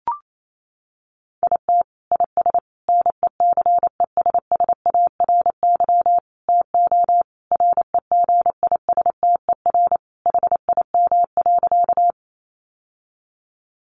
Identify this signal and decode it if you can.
Morse code, heard as a keyed tone, about 28 wpm, 700 Hz. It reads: IT IS NECESSARY TO REGISTER HIM.
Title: Sentences from Top 1200 Words 28wpm